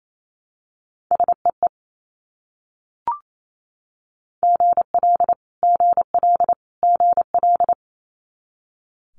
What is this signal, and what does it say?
Morse code, heard as a keyed tone, about 28 wpm, 700 Hz. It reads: SEE GL GL GL
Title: Multi-letter Phonemes Repeated 3x 28wpm